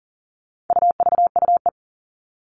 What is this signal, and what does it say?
Morse code, heard as a keyed tone, about 40 wpm, 700 Hz. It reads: UVUE
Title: All Cut Numbers - 4-Digit 40wpm